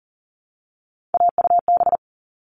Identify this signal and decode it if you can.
Morse code, heard as a keyed tone, about 40 wpm, 700 Hz. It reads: AUB
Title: All Cut Numbers - 3-Digit 40wpm